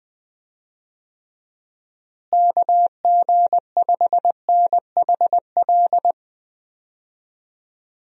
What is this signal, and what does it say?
Morse code, heard as a keyed tone, about 20 wpm, 700 Hz. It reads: KG5NHL
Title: USA Call Signs - 6 Characters - Repeated 2x - 20wpm